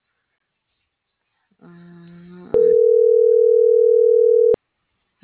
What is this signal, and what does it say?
Ringback tone (North America), 440 + 480 Hz.